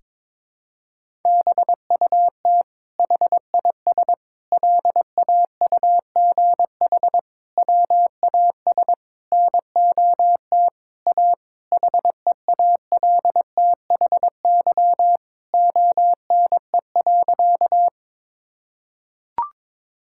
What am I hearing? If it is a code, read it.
Morse code, heard as a keyed tone, about 22 wpm, 700 Hz. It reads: BUT HIS LAUGH WAS NOT A HEALTHY ONE.
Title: Sentences from Top 1000 Words - Review Part 5 - 22wpm